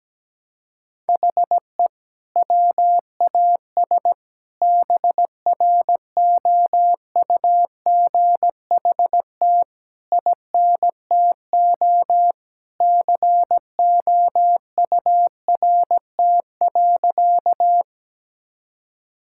Morse code, heard as a keyed tone, about 17 wpm, 700 Hz. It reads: HE WAS BROUGHT INTO COURT.
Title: Sentences from Top 1600 Words 17wpm